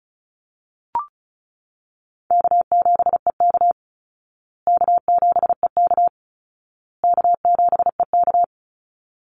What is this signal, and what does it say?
Morse code, heard as a keyed tone, about 34 wpm, 700 Hz. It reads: K7EK K7EK K7EK
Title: Easy Call Signs Repeated 3x - 35wpm